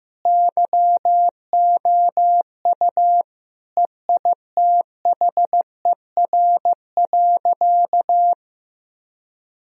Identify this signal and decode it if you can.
Morse code, heard as a keyed tone, about 15 wpm, 700 Hz. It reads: YOU EITHER.
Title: Sentences from Top 1000 Words - Review Part 4 - 15wpm